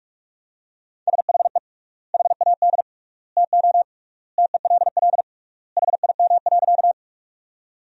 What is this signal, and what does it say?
Morse code, heard as a keyed tone, about 46 wpm, 700 Hz. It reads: SHE HAD TO TELL HIM.
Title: 3x Word-Spacing - Sentences from Top 200 Words - 45wpm